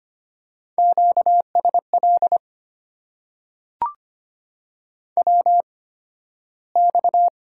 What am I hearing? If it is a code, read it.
Morse code, heard as a keyed tone, about 25 wpm, spaced as for 13 wpm, 690 Hz. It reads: QSL WX
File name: Top 90 QSO Elements - 8x Character Spacing - 25wpm